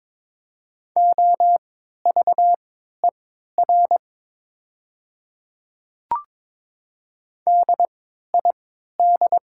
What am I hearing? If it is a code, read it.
Morse code, heard as a keyed tone, about 22 wpm, spaced as for 12 wpm, 700 Hz. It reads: OVER DID
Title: 3x Character Spacing - Top 200 Words - 22wpm